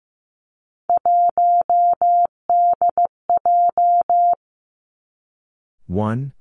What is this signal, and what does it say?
Morse code, heard as a keyed tone, about 15 wpm, 700 Hz. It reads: 1DJ
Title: Three Challenging Letter-Number Combinations 15wpm